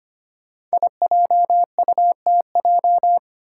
Morse code, heard as a keyed tone, about 25 wpm, 700 Hz. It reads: IJUTJ
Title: Five Challenging Letter-Number Combinations 25wpm